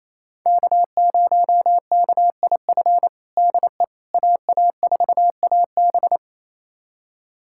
Morse code, heard as a keyed tone, about 28 wpm, 700 Hz. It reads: K0KIF DE AA4AB
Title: USA Callsign DE Callsign - 5 Characters - 28wpm